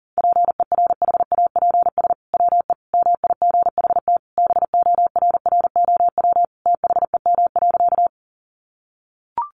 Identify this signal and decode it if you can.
Morse code, heard as a keyed tone, about 40 wpm, 700 Hz. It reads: PERHAPS WE MIGHT BORROW THEM.
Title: Sentences from Top 1500 Words 40wpm